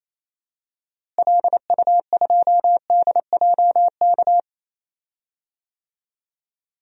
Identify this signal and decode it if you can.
Morse code, heard as a keyed tone, about 28 wpm, 700 Hz. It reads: LU2DJK
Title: DXCC-100 Call Signs - 6 Characters - Repeated 3x - 28wpm